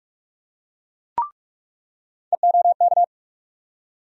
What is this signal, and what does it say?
Morse code, heard as a keyed tone, about 46 wpm, 700 Hz. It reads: EOK
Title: USA Call Sign Postfixes - 3 Letters 45wpm